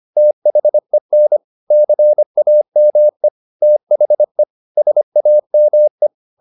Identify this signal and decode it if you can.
Morse code, heard as a keyed tone, about 25 wpm, 600 Hz. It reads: THEN CAME THE SAME
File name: the old man and the sea part 1 25wpm